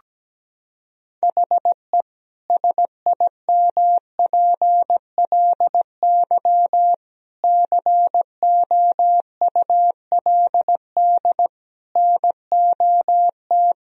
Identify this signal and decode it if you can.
Morse code, heard as a keyed tone, about 17 wpm, 700 Hz. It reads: HE SIMPLY COULD NOT
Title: Sentences from Top 1000 Words - Review Part 4 - 17wpm